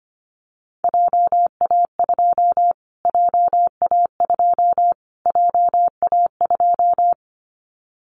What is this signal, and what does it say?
Morse code, heard as a keyed tone, about 25 wpm, 700 Hz. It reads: JA2 JA2 JA2
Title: Callsign Prefixes for Top 25 DXCC - Repeated 3x - 25wpm